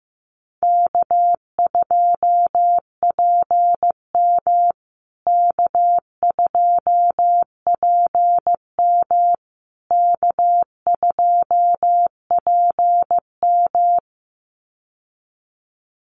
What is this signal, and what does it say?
Morse code, heard as a keyed tone, about 15 wpm, 700 Hz. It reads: K2PM K2PM K2PM
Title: USA Call Signs Repeated 3x 15wpm